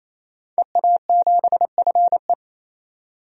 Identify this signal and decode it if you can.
Morse code, heard as a keyed tone, about 28 wpm, 700 Hz. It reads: EA7FE
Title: DXCC-100 Call Signs - 5 Characters - 28wpm